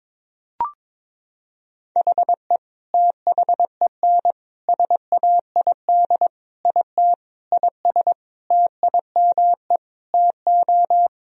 Morse code, heard as a keyed tone, about 22 wpm, 700 Hz. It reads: HE THEN SAID IT IS TIME TO
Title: Sentences from Top 1000 Words - Review Part 1 - 22wpm